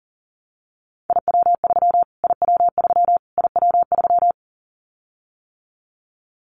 Morse code, heard as a keyed tone, about 40 wpm, 700 Hz. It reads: IW3 IW3 IW3
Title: Callsign Prefixes for Top 25 DXCC - Repeated 3x - 40wpm